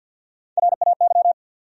Morse code, heard as a keyed tone, about 53 wpm, 700 Hz. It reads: RAY
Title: USA Common Names - 50wpm